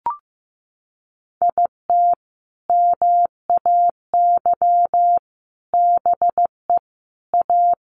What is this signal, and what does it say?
Morse code, heard as a keyed tone, about 15 wpm, 700 Hz. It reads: IT MAY BE A
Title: Sentences from Top 1100 Words 15wpm